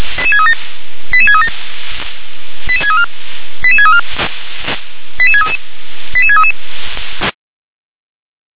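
Five-tone selective call (ZVEI), heard as quick runs of five tones, tones of about 80 ms. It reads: 88637 88634 88634 88633 88633
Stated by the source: ZVEI five-tone selective call received by SDR.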